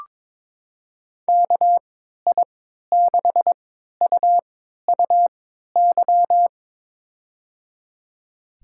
Morse code, heard as a keyed tone, about 22 wpm, spaced as for 12 wpm, 700 Hz. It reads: KI6UUY
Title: USA Call Signs - 6 Characters - 3x Character Spacing - 22wpm